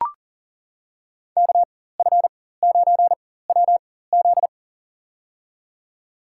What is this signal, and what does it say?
Morse code, heard as a keyed tone, about 38 wpm, spaced as for 18 wpm, 700 Hz. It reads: KF9WZ
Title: USA Call Signs - 5 Characters - 4x Character Spacing - 40wpm